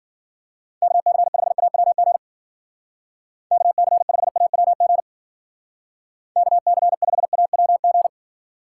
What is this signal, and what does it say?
Morse code, heard as a keyed tone, about 48 wpm, 700 Hz. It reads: KC5AWG KC5AWG KC5AWG
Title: Speed Racing - USA Call Signs 40wpm